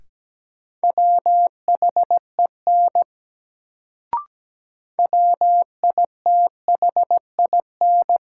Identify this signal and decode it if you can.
Morse code, heard as a keyed tone, about 17 wpm, 700 Hz. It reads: WHEN WITHIN
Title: Sets of 2 Words - New Word Drill - 17wpm